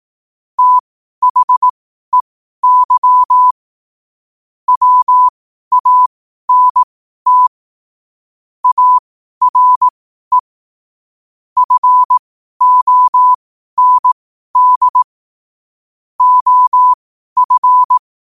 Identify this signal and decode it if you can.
Morse code, heard as a keyed tone, about 18 wpm, spaced as for 12 wpm, 1000 Hz. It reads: THEY WANT ARE FOND OF
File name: qotdfiles / 12wpm